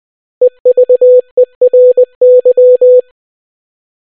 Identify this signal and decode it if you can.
Morse code, heard as a keyed tone, about 20 wpm, 500 Hz. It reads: EVERY